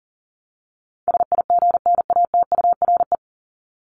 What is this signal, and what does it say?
Morse code, heard as a keyed tone, about 40 wpm, 700 Hz. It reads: SIGNATURE
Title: Top 1751-2000 Words 40wpm